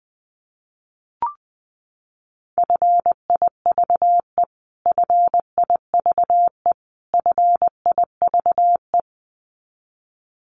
Morse code, heard as a keyed tone, about 20 wpm, 700 Hz. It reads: FIVE FIVE FIVE
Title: Top 500 Words Repeated 20wpm